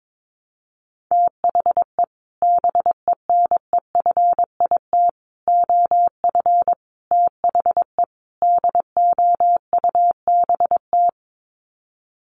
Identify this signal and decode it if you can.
Morse code, heard as a keyed tone, about 22 wpm, 700 Hz. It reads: THE BENEFIT OF THE DOUBT
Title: Sets of 5 Words Encore 22wpm